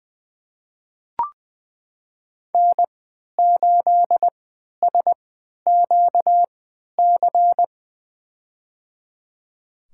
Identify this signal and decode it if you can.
Morse code, heard as a keyed tone, about 21 wpm, spaced as for 11 wpm, 700 Hz. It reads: N8SQC
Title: USA Call Signs - 5 Characters - 3x Character Spacing - 20wpm